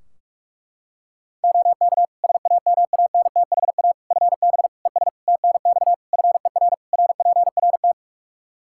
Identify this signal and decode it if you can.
Morse code, heard as a keyed tone, about 46 wpm, 700 Hz. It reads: OK SAMANTHA FB ES TNX FER RPRT
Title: Practice QSO - Exchange 2 - Info - 45wpm